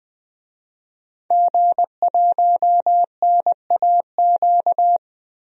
Morse code, heard as a keyed tone, about 20 wpm, 700 Hz. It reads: G1NAQ
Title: DXCC-100 Call Signs - 5 Characters - 20wpm